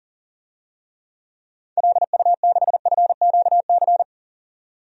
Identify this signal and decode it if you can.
Morse code, heard as a keyed tone, about 37 wpm, 700 Hz. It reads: LU6FQC
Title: DXCC-100 Call Signs - 6 Characters - 40wpm